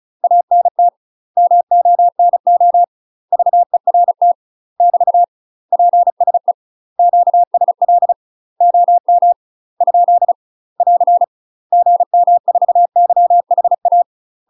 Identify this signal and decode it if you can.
Morse code, heard as a keyed tone, about 36 wpm, 700 Hz. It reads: ANT MONO VERT <BT> PSE QSL OM ? <AR> GM4YHA
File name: UK MASTER QSOs TRAINING / 35WPM / UK MASTER QSOs 0035